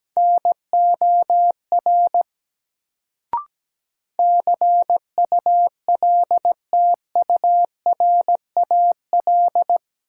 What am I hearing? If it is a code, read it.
Morse code, heard as a keyed tone, about 17 wpm, 700 Hz. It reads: NOR CULTURAL
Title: Sentences from Top 1500 Words - New Word Drill Alt - 17wpm